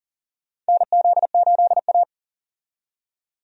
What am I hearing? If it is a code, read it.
Morse code, heard as a keyed tone, about 40 wpm, 700 Hz. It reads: NZ8A
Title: USA Call Signs - 4 Characters - 40wpm